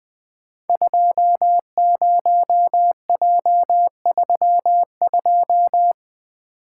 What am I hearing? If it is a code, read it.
Morse code, heard as a keyed tone, about 20 wpm, 700 Hz. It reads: 20J32
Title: Code Groups - 5 Letter:Numbers - 20wpm